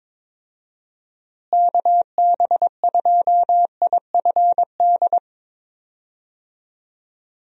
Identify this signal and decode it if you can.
Morse code, heard as a keyed tone, about 21 wpm, 700 Hz. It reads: KB2IFD
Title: USA Call Signs - 6 Characters - Repeated 2x - 22wpm